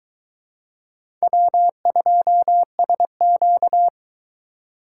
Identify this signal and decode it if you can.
Morse code, heard as a keyed tone, about 23 wpm, 700 Hz. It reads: W2SQ